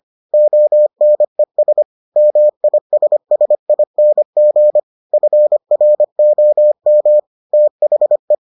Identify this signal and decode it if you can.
Morse code, heard as a keyed tone, about 25 wpm, 600 Hz. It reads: ONES MISSING FROM THE